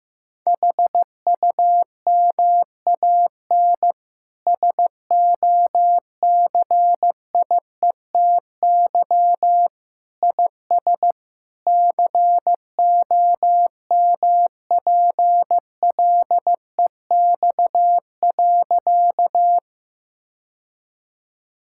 Morse code, heard as a keyed tone, about 16 wpm, 700 Hz. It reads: HUMAN SOCIETY IS COMPLEX.
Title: Sentences from Top 1000 Words - Review Part 6 - 15wpm